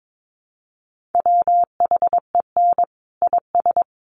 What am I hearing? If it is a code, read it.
Morse code, heard as a keyed tone, about 22 wpm, 700 Hz. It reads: WHEN IS
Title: Sentences from Top 100 Words 22wpm